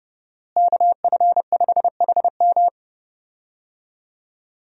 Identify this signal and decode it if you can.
Morse code, heard as a keyed tone, about 30 wpm, 700 Hz. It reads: KF5HM
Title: USA Call Signs - 5 Characters - Repeated 3x - 30wpm